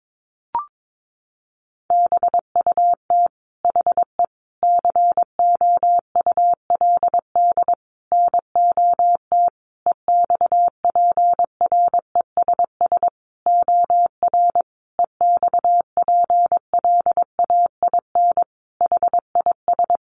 Morse code, heard as a keyed tone, about 22 wpm, 700 Hz. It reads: BUT HE COULD NOT EXPRESS OR EXPLAIN HIS
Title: Sentences from Top 1100 Words 22wpm